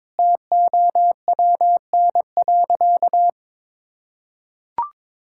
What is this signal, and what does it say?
Morse code, heard as a keyed tone, about 22 wpm, 700 Hz. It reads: TOWN.
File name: Difficult Sentences with Hints - 22wpm